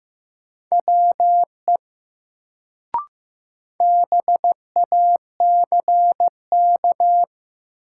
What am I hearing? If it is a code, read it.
Morse code, heard as a keyed tone, about 15 wpm, 700 Hz. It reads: WE BACK
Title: Sets of 2 Words - New Word Drill - 15wpm